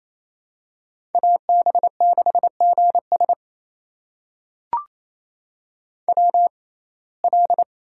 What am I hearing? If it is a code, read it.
Morse code, heard as a keyed tone, about 28 wpm, spaced as for 20 wpm, 700 Hz. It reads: AB6GS WL
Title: USA Call Signs - 6x Character Spacing - 28wpm